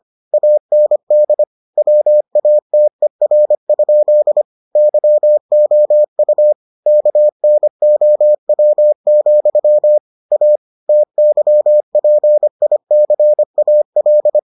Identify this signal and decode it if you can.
Morse code, heard as a keyed tone, about 25 wpm, 600 Hz. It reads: AND WATER? YOU KNOW, A TYPICAL